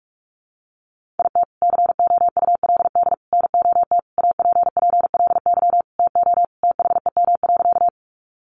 Morse code, heard as a keyed tone, about 44 wpm, 700 Hz. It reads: IT COULD NOT APPLY TO THEM.
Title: Sentences from Top 1200 Words 45wpm